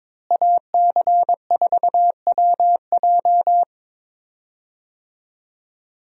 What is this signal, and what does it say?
Morse code, heard as a keyed tone, about 22 wpm, 700 Hz. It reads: AC4WJ
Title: USA Call Signs - 5 Characters - Repeated 2x - 22wpm